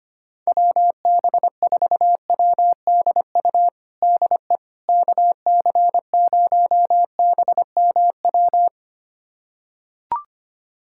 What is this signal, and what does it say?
Morse code, heard as a keyed tone, about 25 wpm, 700 Hz. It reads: WB4WDU DE KC0BMW
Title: USA Callsign DE Callsign - 6 Characters - 25wpm